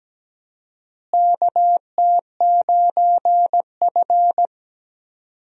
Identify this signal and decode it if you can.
Morse code, heard as a keyed tone, about 17 wpm, 700 Hz. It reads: KT9F
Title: USA Call Signs - 4 Characters - Repeated 2x - 17wpm